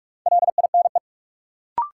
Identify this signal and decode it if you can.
Morse code, heard as a keyed tone, about 46 wpm, 700 Hz. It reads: LINE
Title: Sentences from Top 1000 Words - Review Part 1 - New Word Drill - 45wpm